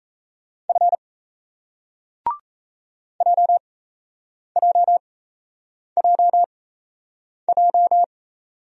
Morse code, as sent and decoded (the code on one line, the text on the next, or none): ..-. / .--- .--- .--- .---
F JJJJ